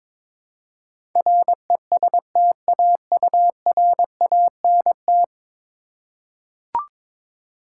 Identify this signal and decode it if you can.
Morse code, heard as a keyed tone, about 22 wpm, 700 Hz. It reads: RESTAURANT
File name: practice / 22wpm / 10 Letter Words - First 3 Letters Given - 22wpm